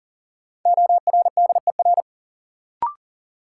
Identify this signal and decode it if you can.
Morse code, heard as a keyed tone, about 40 wpm, 700 Hz. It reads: ORDER
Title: Sets of 2 Words - New Word Drill - 40wpm